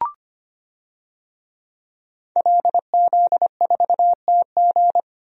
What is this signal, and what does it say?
Morse code, heard as a keyed tone, about 25 wpm, 700 Hz. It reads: LZ4TG
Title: DXCC-100 Call Signs - 5 Characters - Repeated 2x - 25wpm